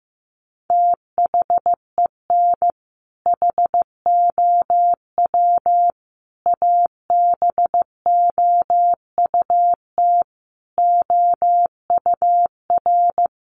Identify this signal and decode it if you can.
Morse code, heard as a keyed tone, about 15 wpm, 700 Hz. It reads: THEN HOW ABOUT OUR
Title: Sentences from Top 900 Words 15wpm